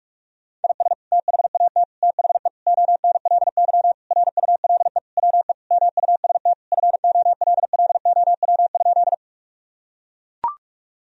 Morse code, heard as a keyed tone, about 41 wpm, 700 Hz. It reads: IS THAT THE ONLY RULE WE MUST FOLLOW?
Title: Sentences from Top 1000 Words - Review Part 2 - 45wpm